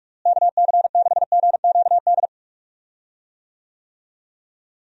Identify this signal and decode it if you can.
Morse code, heard as a keyed tone, about 45 wpm, 700 Hz. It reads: KC6GQD
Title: USA Call Signs - 6 Characters - Repeated 2x - 45wpm